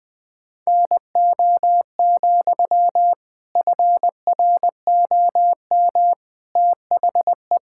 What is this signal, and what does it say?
Morse code, heard as a keyed tone, about 20 wpm, 700 Hz. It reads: NO, FROM THE
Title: Sentences from Top 1000 Words - Review Part 6 - 20wpm